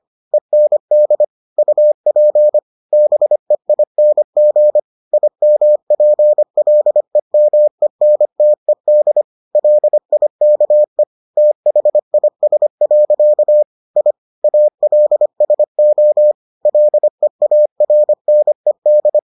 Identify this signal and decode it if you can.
Morse code, heard as a keyed tone, about 25 wpm, 600 Hz. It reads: END UP BEING IMPLEMENTED LIKE THIS. I ALSO LEARNED